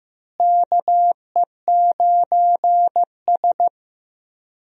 Morse code, heard as a keyed tone, about 15 wpm, 700 Hz. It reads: KE9S
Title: USA Call Signs - 4 Characters - Repeated 2x - 15wpm